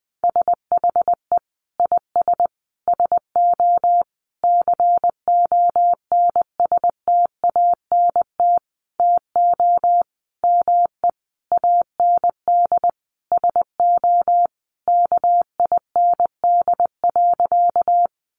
Morse code, heard as a keyed tone, about 19 wpm, 700 Hz. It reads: SHE IS SO CONSTANT TO ME AND SO KIND.
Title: Sentences from Top 1500 Words 20wpm